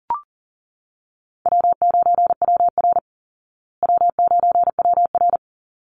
Morse code, heard as a keyed tone, about 40 wpm, 700 Hz. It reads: W9WR W9WR
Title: Call Signs Repeated 3x - 40wpm